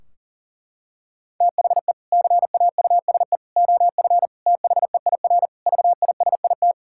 Morse code, heard as a keyed tone, about 40 wpm, 700 Hz. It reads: THE CAUSE OF THEIR VISIT
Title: Sentences from Top 1600 Words 40wpm